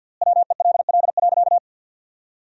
Morse code, heard as a keyed tone, about 48 wpm, 700 Hz. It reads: WELL.
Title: 4x Word-Spacing - Sentences from Top 500 Words - 50wpm